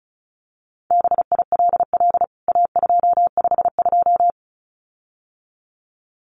Morse code, heard as a keyed tone, about 34 wpm, 700 Hz. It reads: BILL A252
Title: CWT Exchange 35wpm